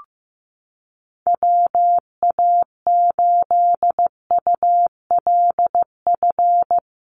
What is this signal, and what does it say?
Morse code, heard as a keyed tone, about 15 wpm, 700 Hz. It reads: WA8ULF
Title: USA Call Signs Repeated 2x 15wpm